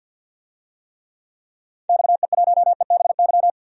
Morse code, heard as a keyed tone, about 52 wpm, 700 Hz. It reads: XE1EBY
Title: DXCC-100 Call Signs - 6 Characters - 50wpm